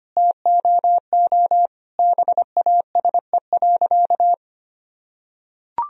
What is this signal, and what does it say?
Morse code, heard as a keyed tone, about 25 wpm, 700 Hz. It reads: TOO BASE.
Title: Sentences from Top 1000 Words - Review Part 1 - 25wpm